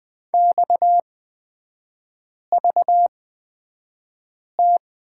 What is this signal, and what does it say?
Morse code, heard as a keyed tone, about 20 wpm, spaced as for 5 wpm, 700 Hz. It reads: XVT
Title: 229 Lesson 37e - Letter x - Three Characters 20wpm